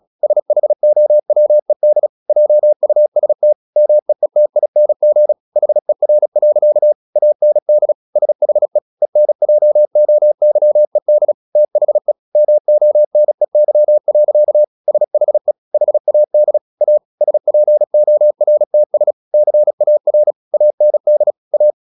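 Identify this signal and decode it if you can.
Morse code, heard as a keyed tone, about 36 wpm, 600 Hz. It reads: SHOWED JUST MEETING HER. AND SHE ENJOYED THE MONEY. SHE HAD A SPORTS CAR AND A
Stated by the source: Morse practice at 36 wpm, text of Ken Follett's novel Never, from CW med Gnister.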